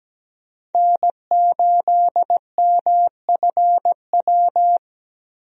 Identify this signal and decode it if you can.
Morse code, heard as a keyed tone, about 17 wpm, 700 Hz. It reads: N8MFW
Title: USA Call Signs - 5 Characters - Repeated 3x - 17wpm